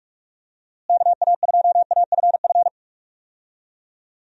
Morse code, heard as a keyed tone, about 46 wpm, 700 Hz. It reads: KA2AFF
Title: USA Call Signs - 6 Characters - Repeated 3x - 45wpm